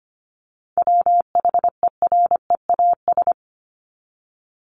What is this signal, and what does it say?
Morse code, heard as a keyed tone, about 25 wpm, 700 Hz. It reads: WHEREAS
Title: Top 1501-1750 Words 25wpm